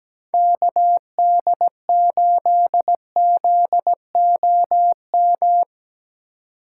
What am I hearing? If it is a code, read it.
Morse code, heard as a keyed tone, about 17 wpm, 700 Hz. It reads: KD8ZOM
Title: USA Call Signs - 6 Characters - 4x Character Spacing - 17wpm